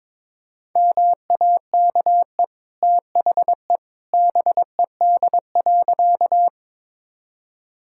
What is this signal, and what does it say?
Morse code, heard as a keyed tone, about 22 wpm, 690 Hz. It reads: MAKE THE BED.
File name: practice / 22wpm / Easy Sentences with Hints - 22wpm